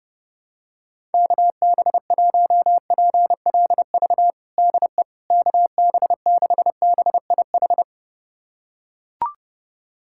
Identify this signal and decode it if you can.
Morse code, heard as a keyed tone, about 29 wpm, 700 Hz. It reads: KB1PLV DE KB6BIH
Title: USA Callsign DE Callsign - 6 Characters - 30wpm